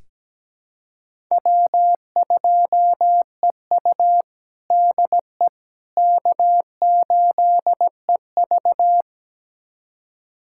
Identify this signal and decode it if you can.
Morse code, heard as a keyed tone, about 17 wpm, 700 Hz. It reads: W2EU DE K8EV
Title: USA Callsign DE Callsign - 4 Characters - 17wpm